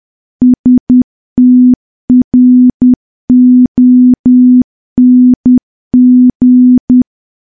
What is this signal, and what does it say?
Morse code, heard as a keyed tone, about 10 wpm, 250 Hz. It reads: STRONG
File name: morse / morse-training-10WPM